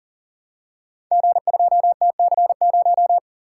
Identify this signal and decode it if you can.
Morse code, heard as a keyed tone, about 39 wpm, 700 Hz. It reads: G2TC0
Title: Code Groups - 5 Letter:Numbers - 40wpm